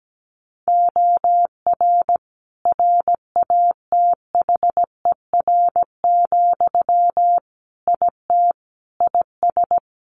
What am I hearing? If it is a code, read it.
Morse code, heard as a keyed tone, about 17 wpm, 700 Hz. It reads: OR RATHER, IT IS
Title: Sentences from Top 1400 Words 17wpm